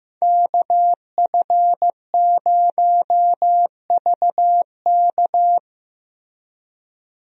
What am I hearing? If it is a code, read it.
Morse code, heard as a keyed tone, about 15 wpm, 700 Hz. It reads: KF0VK
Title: USA Call Signs - 5 Characters - Repeated 3x - 15wpm